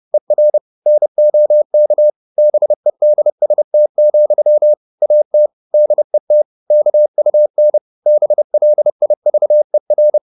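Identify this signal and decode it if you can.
Morse code, heard as a keyed tone, about 30 wpm, 600 Hz. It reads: ER NOK BEDST, AT DET KUN BLIVER
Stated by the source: High-speed Morse (QRQ) at 30 wpm.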